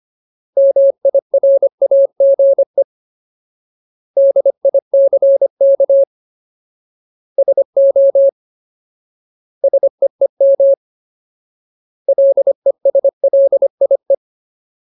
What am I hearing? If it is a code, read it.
Morse code, heard as a keyed tone, about 25 wpm, 550 Hz. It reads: MIRAGE DICK SO SEEM LESLIE
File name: Ultimate Word List / 25WPM Ultimate